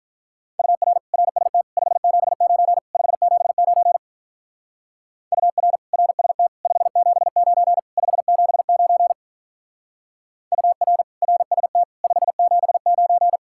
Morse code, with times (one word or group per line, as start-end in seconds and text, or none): 0.6-1.0 UR
1.1-1.6 RST
1.8-2.8 579
2.9-4.0 579
5.3-5.8 UR
5.9-6.5 RST
6.7-7.8 579
8.0-9.1 579
10.5-11.0 UR
11.2-11.8 RST
12.0-13.4 579